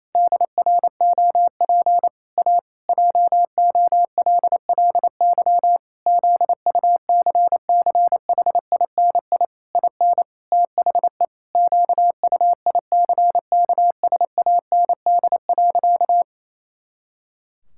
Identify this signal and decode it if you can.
Morse code, heard as a keyed tone, about 28 wpm, 700 Hz. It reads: DROP A JOLLY ZUCCHINI IN THE QUICKSAND.
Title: Pangram Challenge v2 - 28wpm